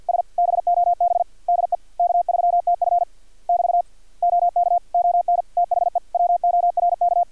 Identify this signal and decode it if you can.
Morse code, heard as a keyed tone, about 50 wpm, 700 Hz. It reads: S BOB DE K3TF <BT> OK ON THE WORK
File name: archived / CRSnet